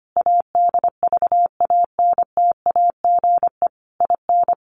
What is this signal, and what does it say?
Morse code, heard as a keyed tone, about 25 wpm, 700 Hz. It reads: ADVANTAGE IN
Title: Sentences from Top 900 Words 25wpm